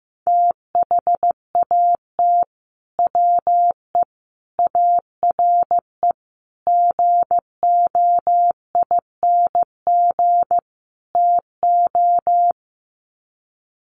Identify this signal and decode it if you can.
Morse code, heard as a keyed tone, about 15 wpm, 700 Hz. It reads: THAT WE ARE GOING TO
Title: Sets of 5 Words Encore 15wpm